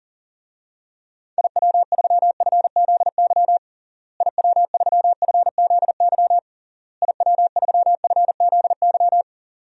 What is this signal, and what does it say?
Morse code, heard as a keyed tone, about 40 wpm, 700 Hz. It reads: IW3FZY IW3FZY IW3FZY
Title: DXCC-100 Call Signs - 6 Characters - Repeated 3x - 40wpm